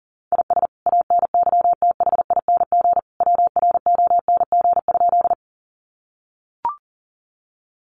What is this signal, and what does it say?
Morse code, heard as a keyed tone, about 40 wpm, 700 Hz. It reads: IS ANYTHING WRONG?
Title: Sentences from Top 1000 Words 40wpm